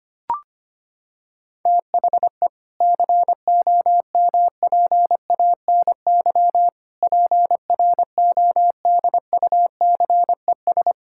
Morse code, heard as a keyed tone, about 25 wpm, 700 Hz. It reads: THE COMPANY PRODUCES
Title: Sentences from Top 1000 Words - Review Part 6 - 25wpm